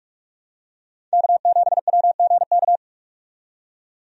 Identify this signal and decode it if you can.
Morse code, heard as a keyed tone, about 47 wpm, 700 Hz. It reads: K7WGK
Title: USA Call Signs - 5 Characters - 45wpm